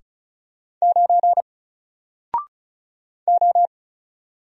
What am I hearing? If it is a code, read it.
Morse code, heard as a keyed tone, about 34 wpm, 700 Hz. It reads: 9 O
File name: Single-Letter-Number-35wpm